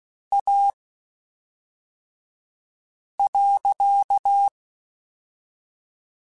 Morse code, heard as a keyed tone, about 16 wpm, spaced as for 3 wpm, 800 Hz. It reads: A.
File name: (spaced) / QSOs Spaced-mp3s / kc7sac @5 sp